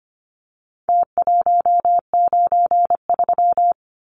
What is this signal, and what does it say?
Morse code, heard as a keyed tone, about 25 wpm, 700 Hz. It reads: T193
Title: Numbers - Contest Style - 4-Digit 25wpm